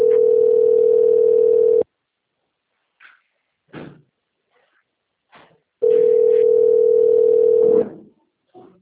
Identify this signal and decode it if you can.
Ringback tone (North America), 440 + 480 Hz.